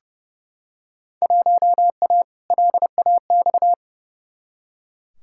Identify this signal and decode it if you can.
Morse code, heard as a keyed tone, about 30 wpm, 700 Hz. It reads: 1A LAX
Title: Field Day Exchanges Encore - 30wpm